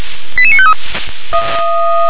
Five-tone selective call (ZVEI), heard as quick runs of five tones, tones of about 70 ms. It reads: 88953
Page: ZVEI